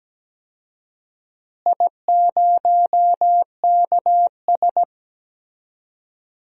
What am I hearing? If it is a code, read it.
Morse code, heard as a keyed tone, about 17 wpm, 700 Hz. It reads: I0KS
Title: DXCC-100 Call Signs - 4 Characters - Repeated 2x - 17wpm